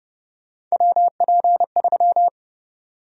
Morse code, heard as a keyed tone, about 32 wpm, 700 Hz. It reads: WP3
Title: USA Call Sign Prefixes - Repeated 2x - 30wpm